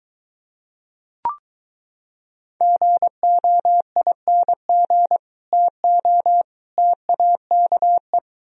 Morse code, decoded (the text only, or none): GOING TO TAKE